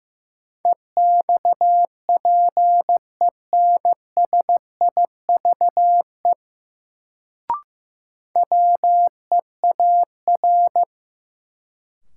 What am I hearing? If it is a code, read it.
Morse code, heard as a keyed tone, about 15 wpm, 700 Hz. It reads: EXPENSIVE WEAR
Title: Sentences from Top 1000 Words - Review Part 6 - New Word Drill - 15wpm